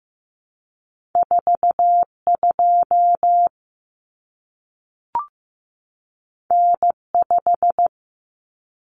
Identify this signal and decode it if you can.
Morse code, heard as a keyed tone, about 15 wpm, 700 Hz. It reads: 42 N5
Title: Common Cut Numbers - 2-Digit 15wpm